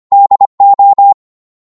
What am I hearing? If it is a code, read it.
Morse code, heard as a keyed tone, about 25 wpm, 800 Hz. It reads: DO